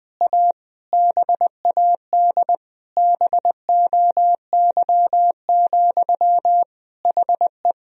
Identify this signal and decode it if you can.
Morse code, heard as a keyed tone, about 20 wpm, 700 Hz. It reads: A BAD BOY, HE
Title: Sentences from Top 1000 Words - Review Part 3 - 20wpm